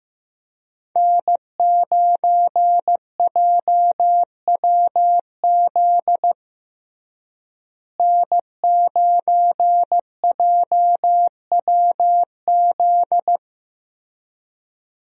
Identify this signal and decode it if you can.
Morse code, heard as a keyed tone, about 15 wpm, 700 Hz. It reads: N9JWZ N9JWZ
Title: USA Call Signs - 5 Characters - Repeated 2x - 15wpm